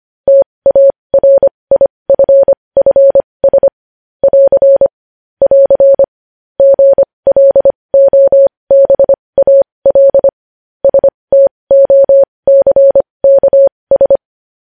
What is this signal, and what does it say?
Morse code, heard as a keyed tone, about 25 wpm, spaced as for 20 wpm, 550 Hz. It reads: TARIFFS <AR> <AR> GLOBAL STOCKS
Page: news headlines in morse code at 20 wpm